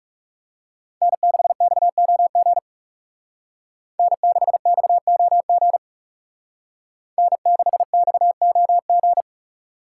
Morse code, heard as a keyed tone, about 35 wpm, 700 Hz. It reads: N6XOG N6XOG N6XOG
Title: Speed Racing - USA 1x3 Call Signs 30wpm